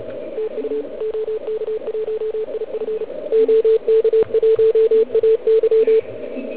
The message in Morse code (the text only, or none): TU OK1IF OK1AY